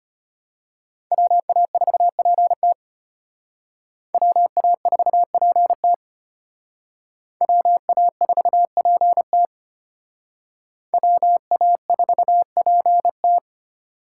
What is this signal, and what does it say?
Morse code, heard as a keyed tone, about 32 wpm, 700 Hz. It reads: WA4PT WA4PT WA4PT WA4PT